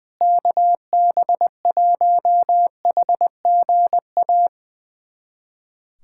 Morse code, heard as a keyed tone, about 21 wpm, 700 Hz. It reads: KB1HGA
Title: USA Callsign DE Callsign - 6 Characters - 20wpm